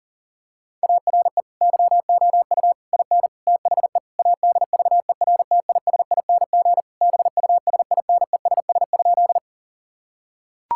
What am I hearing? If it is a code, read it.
Morse code, heard as a keyed tone, about 40 wpm, 700 Hz. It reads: ARE YOU IN THE ADVERTISING BUSINESS?
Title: Sentences from Top 1000 Words - Review Part 6 - 40wpm